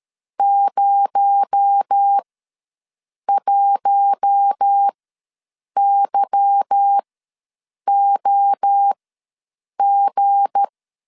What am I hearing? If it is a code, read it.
Morse code, heard as a keyed tone, about 13 wpm, spaced as for 7 wpm, 800 Hz. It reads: 01YOG
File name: LEZIONI / 8 YJ45 / 8 Y ws 12wpm 3409